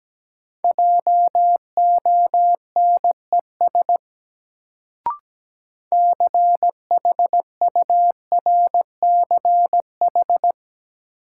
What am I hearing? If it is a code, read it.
Morse code, heard as a keyed tone, about 17 wpm, 700 Hz. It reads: JONES CHURCH
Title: Sets of 5 Words - New Word Drill - 17wpm